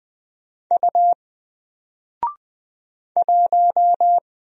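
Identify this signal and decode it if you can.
Morse code, heard as a keyed tone, about 20 wpm, 700 Hz. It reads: U 1
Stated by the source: Single letter and number Morse drill at 20 wpm, from N4LCD.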